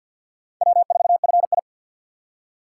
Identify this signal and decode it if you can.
Morse code, heard as a keyed tone, about 64 wpm, 700 Hz. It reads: W4FI